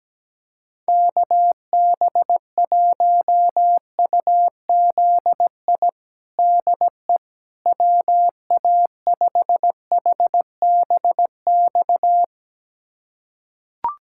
Morse code, heard as a keyed tone, about 17 wpm, 700 Hz. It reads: KB1UZI DE WA5HBX
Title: USA Callsign DE Callsign 17wpm